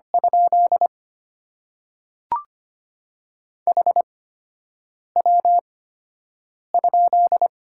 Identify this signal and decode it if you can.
Morse code, heard as a keyed tone, about 25 wpm, spaced as for 7 wpm, 700 Hz. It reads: ? HW ?
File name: Top 90 QSO Elements - 8x Character Spacing - 25wpm